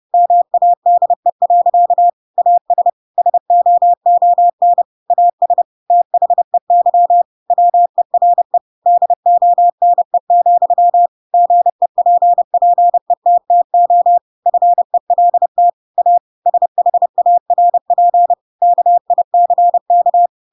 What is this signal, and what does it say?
Morse code, heard as a keyed tone, about 30 wpm, 700 Hz. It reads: MADE. AS SOON AS THEY WERE DONE, GEPPETTO FELT A SHARP KICK
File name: Pinocchio / Pinocchio ch3 30 30 ik1hge